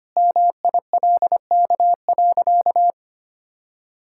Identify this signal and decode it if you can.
Morse code, heard as a keyed tone, about 25 wpm, 690 Hz. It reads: MILK.
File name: Easy Sentences with Hints - 25wpm